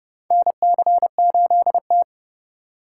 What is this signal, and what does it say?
Morse code, heard as a keyed tone, about 29 wpm, 700 Hz. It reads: NC8T
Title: USA Call Signs - 4 Characters - 2x Character Spacing - 30wpm